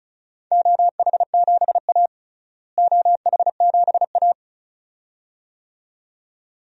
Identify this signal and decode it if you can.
Morse code, heard as a keyed tone, about 34 wpm, 700 Hz. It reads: OH7A OH7A
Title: DXCC-100 Call Signs - 4 Characters - Repeated 2x - 35wpm